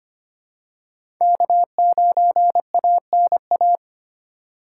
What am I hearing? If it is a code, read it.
Morse code, heard as a keyed tone, about 25 wpm, 700 Hz. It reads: K9ANA
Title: USA Call Signs - 5 Characters - 4x Character Spacing - 25wpm